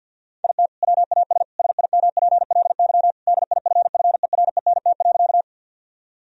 Morse code, heard as a keyed tone, about 50 wpm, 700 Hz. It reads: IT WAS SIMPLY DIFFERENT.
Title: Sentences from Top 1000 Words - Review Part 4 - 50wpm